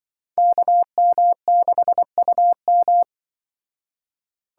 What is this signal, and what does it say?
Morse code, heard as a keyed tone, about 25 wpm, 700 Hz. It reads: KM6UM
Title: Speed Racing - USA 2x2 Call Signs 20wpm